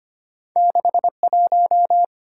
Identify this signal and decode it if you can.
Morse code, heard as a keyed tone, about 25 wpm, 700 Hz. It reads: 61
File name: Two Letter-Number Combinations - 25wpm